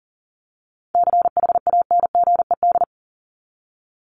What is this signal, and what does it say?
Morse code, heard as a keyed tone, about 40 wpm, 700 Hz. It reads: CHANGED
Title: ED Suffix - Top 100 Words with Base Words 40wpm